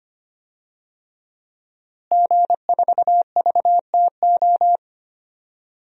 Morse code, heard as a keyed tone, about 25 wpm, 700 Hz. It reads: G4VTO
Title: DXCC-100 Call Signs - 5 Characters - Repeated 3x - 25wpm